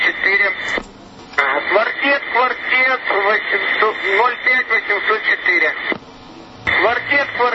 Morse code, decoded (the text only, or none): none